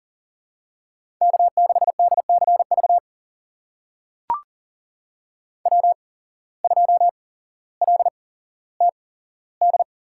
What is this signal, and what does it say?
Morse code, heard as a keyed tone, about 40 wpm, spaced as for 17 wpm, 700 Hz. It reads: K6DCV W2LTD
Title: USA Call Signs - 5 Characters - 8x Character Spacing - 40wpm